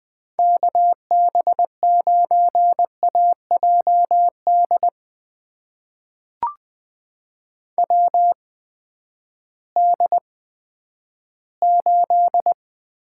Morse code, heard as a keyed tone, about 20 wpm, spaced as for 11 wpm, 700 Hz. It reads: KB9AJD WD8
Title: USA Call Signs - 6 Characters - 8x Character Spacing - 20wpm